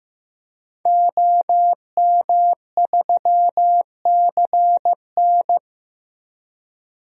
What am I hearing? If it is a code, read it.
Morse code, heard as a keyed tone, about 15 wpm, 700 Hz. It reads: OM3CN
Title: DXCC-100 Call Signs - 5 Characters - 15wpm